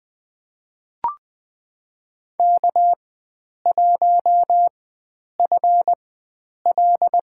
Morse code, heard as a keyed tone, about 20 wpm, spaced as for 9 wpm, 700 Hz. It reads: K1FL
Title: USA Call Signs - 4 Characters - 4x Character Spacing - 20wpm